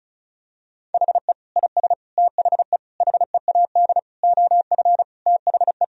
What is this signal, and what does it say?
Morse code, heard as a keyed tone, about 34 wpm, 700 Hz. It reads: HE IS THE HEAD OF THE
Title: Sentences from Top 1000 Words - Review Part 4 - 35wpm